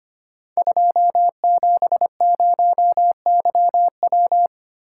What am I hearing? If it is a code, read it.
Morse code, heard as a keyed tone, about 25 wpm, 700 Hz. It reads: 270YW
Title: Code Groups - 5 Letter:Numbers - 25wpm